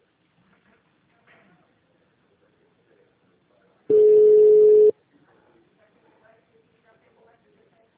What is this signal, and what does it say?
Ringback tone (Europe, 425 Hz), with speech.
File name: out- -2020- - - .28983